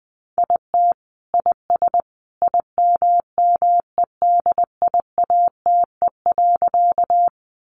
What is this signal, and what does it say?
Morse code, heard as a keyed tone, about 20 wpm, 700 Hz. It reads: IT IS IMMEDIATE.
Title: Sentences from Top 1200 Words 20wpm